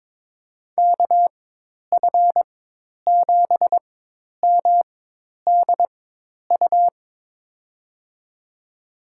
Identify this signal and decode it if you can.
Morse code, heard as a keyed tone, about 22 wpm, spaced as for 10 wpm, 700 Hz. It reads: KF7MDU
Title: USA Call Signs - 6 Characters - 4x Character Spacing - 22wpm